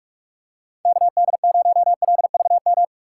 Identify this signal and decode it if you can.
Morse code, heard as a keyed tone, about 42 wpm, 700 Hz. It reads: KD0LVM
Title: USA Call Signs - 6 Characters - 2x Character Spacing - 45wpm